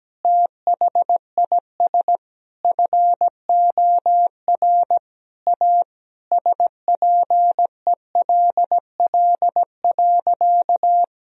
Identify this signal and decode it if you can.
Morse code, heard as a keyed tone, about 17 wpm, 700 Hz. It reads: THIS FOR A SPELL.
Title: Sentences from Top 1000 Words - Review Part 1 - 17wpm